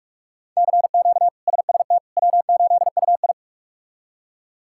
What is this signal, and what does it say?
Morse code, heard as a keyed tone, about 46 wpm, 700 Hz. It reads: CQ SST W8UI
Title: K1USN SST QSO Practice - 45wpm